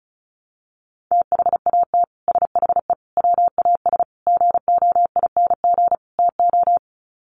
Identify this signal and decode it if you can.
Morse code, heard as a keyed tone, about 34 wpm, 700 Hz. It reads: THAT SHE WAS GOING TO
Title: Sets of 5 Words Encore2 35wpm